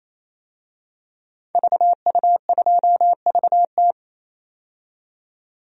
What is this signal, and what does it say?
Morse code, heard as a keyed tone, about 28 wpm, 700 Hz. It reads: VU2VT
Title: DXCC-100 Call Signs - 5 Characters - 28wpm